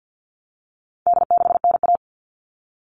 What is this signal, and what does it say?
Morse code, heard as a keyed tone, about 50 wpm, 700 Hz. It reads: D6NA
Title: All Cut Numbers - 4-Digit 50wpm